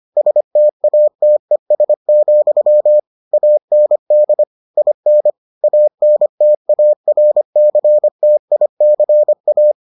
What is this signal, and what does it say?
Morse code, heard as a keyed tone, about 25 wpm, 600 Hz. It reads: STATES, AND IN ANTARCTICA